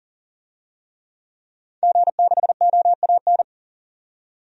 Morse code, heard as a keyed tone, about 40 wpm, 700 Hz. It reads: G6OAN
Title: DXCC-100 Call Signs - 5 Characters - 40wpm